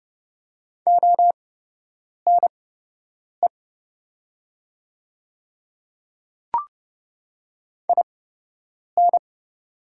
Morse code, heard as a keyed tone, about 30 wpm, spaced as for 8 wpm, 700 Hz. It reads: ONE IN
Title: 8x Character Spacing - Top 100 Words - 30wpm